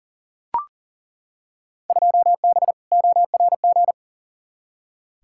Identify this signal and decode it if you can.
Morse code, heard as a keyed tone, about 40 wpm, 700 Hz. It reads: 2B ORG
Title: Field Day Exchanges Encore - 40wpm